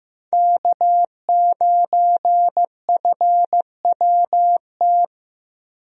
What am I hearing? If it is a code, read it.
Morse code, heard as a keyed tone, about 14 wpm, 700 Hz. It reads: K9FWT
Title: USA Call Signs - 5 Characters - Repeated 3x - 15wpm